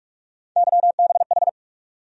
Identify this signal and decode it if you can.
Morse code, heard as a keyed tone, about 46 wpm, 700 Hz. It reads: YBH
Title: USA Call Sign Postfixes - 3 Letters 45wpm